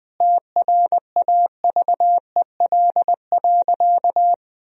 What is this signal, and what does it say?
Morse code, heard as a keyed tone, about 21 wpm, 700 Hz. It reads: TRAVEL.
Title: Sentences from Top 400 Words and Abbreviations 20wpm